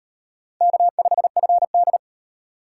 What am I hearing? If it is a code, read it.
Morse code, heard as a keyed tone, about 33 wpm, 700 Hz. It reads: K5FD